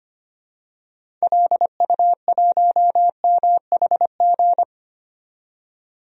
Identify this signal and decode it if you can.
Morse code, heard as a keyed tone, about 25 wpm, 700 Hz. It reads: LU1MHG
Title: DXCC-100 Call Signs - 6 Characters - 25wpm